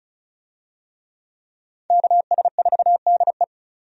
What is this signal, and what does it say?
Morse code, heard as a keyed tone, about 36 wpm, 700 Hz. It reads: KS4DE
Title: USA Call Signs - 5 Characters - 35wpm